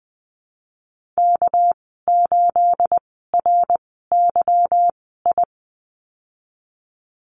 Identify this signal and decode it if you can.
Morse code, heard as a keyed tone, about 20 wpm, spaced as for 15 wpm, 700 Hz. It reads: K8RYI
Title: 213 Lesson 34g - Number 8 - Call Signs 20wpm